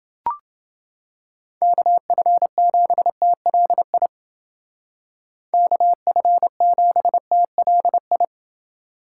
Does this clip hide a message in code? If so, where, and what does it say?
Yes: Morse code from 1.6 to 8.2 s, heard as a keyed tone, about 28 wpm, 700 Hz. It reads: KF7TLI KF7TLI